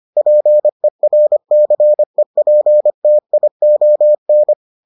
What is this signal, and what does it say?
Morse code, heard as a keyed tone, about 25 wpm, 600 Hz. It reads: PERCEPTION